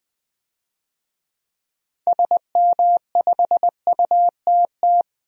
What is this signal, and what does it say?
Morse code, heard as a keyed tone, about 20 wpm, 700 Hz. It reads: SM5UTT
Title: DXCC-100 Call Signs - 6 Characters - 20wpm